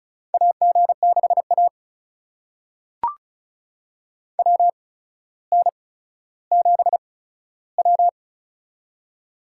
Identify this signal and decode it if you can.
Morse code, heard as a keyed tone, about 36 wpm, spaced as for 15 wpm, 700 Hz. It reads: AG6A WN7W
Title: USA Call Signs - 4 Characters - 8x Character Spacing - 35wpm